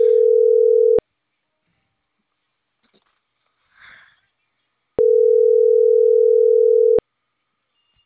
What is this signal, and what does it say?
Ringback tone (North America), 440 + 480 Hz.